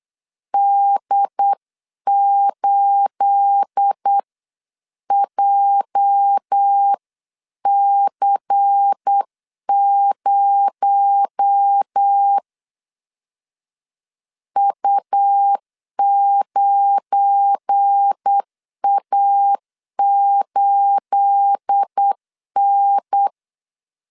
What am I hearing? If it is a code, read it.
Morse code, heard as a keyed tone, about 8 wpm, 800 Hz. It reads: D8JC0 U9A8N
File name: CW / DET / 1 9 1500C S C 8wpm 5653